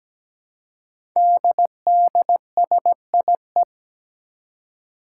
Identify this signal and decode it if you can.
Morse code, heard as a keyed tone, about 17 wpm, 700 Hz. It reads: DDSIE
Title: Five Challenging Letter-Number Combinations 17wpm